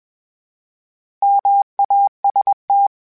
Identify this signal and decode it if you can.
Morse code, heard as a keyed tone, about 21 wpm, 800 Hz. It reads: MAST
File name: Misc / Misc-mp3s / Misc @20